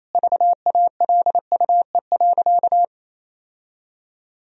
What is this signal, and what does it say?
Morse code, heard as a keyed tone, about 28 wpm, 700 Hz. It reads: VALUE.
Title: Sentences from Top 1000 Words - Review Part 3 - 28wpm